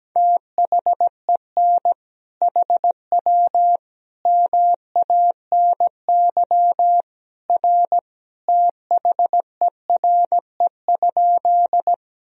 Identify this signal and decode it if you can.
Morse code, heard as a keyed tone, about 17 wpm, 700 Hz. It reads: THEN HW MANY R THERE?
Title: Sentences from Top 100 Words and Abbreviations 17wpm